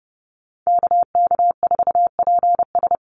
Morse code, heard as a keyed tone, about 30 wpm, 700 Hz. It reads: KK4PS
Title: USA Call Signs Repeated 3x 30wpm